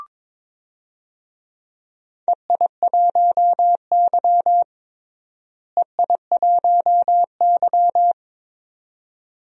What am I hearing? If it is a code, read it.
Morse code, heard as a keyed tone, about 22 wpm, 700 Hz. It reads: EI1Y EI1Y
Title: DXCC-100 Call Signs - 4 Characters - Repeated 2x - 22wpm